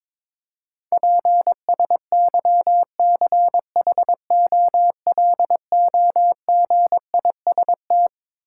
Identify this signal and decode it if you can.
Morse code, heard as a keyed tone, about 22 wpm, 700 Hz. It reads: PSYCHOLOGIST
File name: Twelve Letter Words - 22wpm